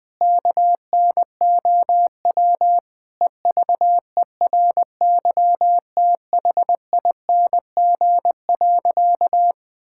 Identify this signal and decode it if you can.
Morse code, heard as a keyed tone, about 20 wpm, 700 Hz. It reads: KNOW EVERYTHING.
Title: Sentences from Top 1600 Words 20wpm